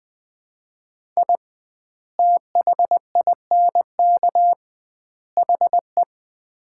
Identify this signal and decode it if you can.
Morse code, heard as a keyed tone, about 20 wpm, 700 Hz. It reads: I THINK HE
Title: 2x Word-Spacing - Sets of 3 Words - 20wpm